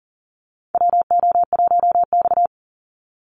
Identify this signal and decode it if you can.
Morse code, heard as a keyed tone, about 40 wpm, 700 Hz. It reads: WO1X
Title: Call Signs Repeated 2x - 40wpm